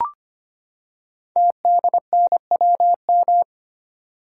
Morse code, heard as a keyed tone, about 25 wpm, 700 Hz. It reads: TDNWM
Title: Code Groups - 5 Letters - 25wpm